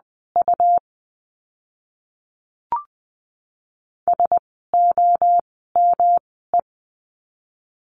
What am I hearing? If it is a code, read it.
Morse code, heard as a keyed tone, about 20 wpm, spaced as for 14 wpm, 700 Hz. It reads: U SOME
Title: 244 Lesson 39f - Letter z - Words 20wpm